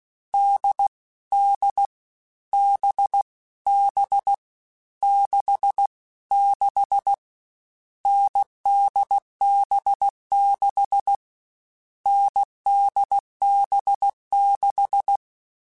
Morse code, heard as a keyed tone, about 16 wpm, 800 Hz. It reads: D D B B 6 6 NDB6 NDB6
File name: Difficult Combos / Difficult Combos-mp3s / awj1 ndb6 gz7 @15